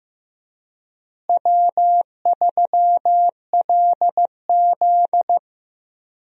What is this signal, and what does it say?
Morse code, heard as a keyed tone, about 15 wpm, 700 Hz. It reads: W3LZ
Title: USA Call Signs - 4 Characters - Repeated 3x - 15wpm